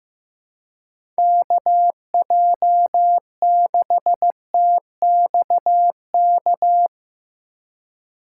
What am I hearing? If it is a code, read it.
Morse code, heard as a keyed tone, about 15 wpm, 700 Hz. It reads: KJ6TXK
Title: USA Call Signs - 6 Characters - 2x Character Spacing - 15wpm